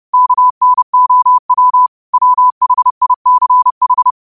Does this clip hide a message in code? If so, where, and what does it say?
Yes: Morse code from 0.1 to 4.1 s, heard as a keyed tone, about 30 wpm, 1000 Hz. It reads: KNOW WHICH